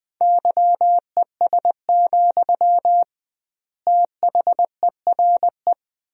Morse code, heard as a keyed tone, about 20 wpm, 700 Hz. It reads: YES, THERE
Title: 2x Word-Spacing - Sentences from Top 500 Words - 20wpm